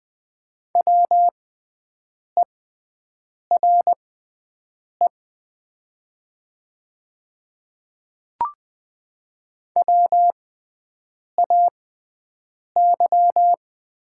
Morse code, heard as a keyed tone, about 20 wpm, spaced as for 7 wpm, 700 Hz. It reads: WERE WAY
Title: 6x Character Spacing - Top 100 Words - 20wpm